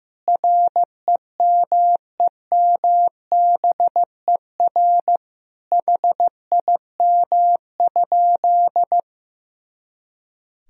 Morse code, heard as a keyed tone, about 15 wpm, 700 Hz. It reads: REMEMBER HIM?
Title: Sentences from Top 400 Words and Abbreviations 15wpm